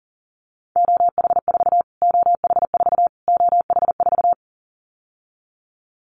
Morse code, heard as a keyed tone, about 40 wpm, 700 Hz. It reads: OH4 OH4 OH4
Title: Callsign Prefixes for Top 25 DXCC - Repeated 3x - 40wpm